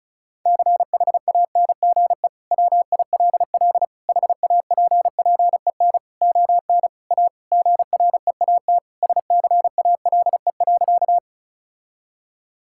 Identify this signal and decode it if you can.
Morse code, heard as a keyed tone, about 36 wpm, 700 Hz. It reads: CHANGE WILL HAPPEN ON A GREAT SCALE.
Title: Sentences from Top 1000 Words - Review Part 5 - 35wpm